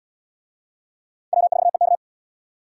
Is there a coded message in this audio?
Yes: Morse code from 1.3 to 1.9 s, heard as a keyed tone, about 65 wpm, 700 Hz. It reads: W4EK